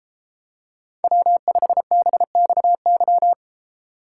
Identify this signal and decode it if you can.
Morse code, heard as a keyed tone, about 34 wpm, 700 Hz. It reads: W5BXY